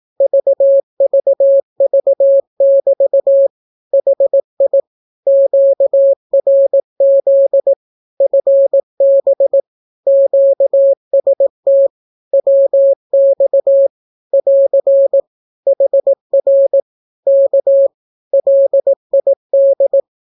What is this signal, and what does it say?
Morse code, heard as a keyed tone, about 18 wpm, 550 Hz. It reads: VVV<BT> HI QRZ FB QST WX <AR> HR K LID
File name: naukapodstaw / skroty / skroty